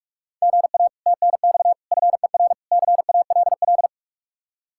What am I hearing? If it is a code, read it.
Morse code, heard as a keyed tone, about 48 wpm, 700 Hz. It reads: GA TNX FER CALL